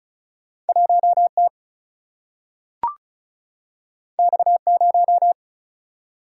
Morse code, heard as a keyed tone, about 36 wpm, 700 Hz. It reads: 1 T X 0
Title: Single-Letter Single-Number 35wpm